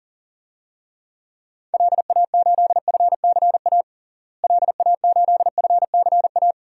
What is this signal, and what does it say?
Morse code, heard as a keyed tone, about 40 wpm, 700 Hz. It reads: LA8FCA LA8FCA
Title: DXCC-100 Call Signs - 6 Characters - Repeated 2x - 40wpm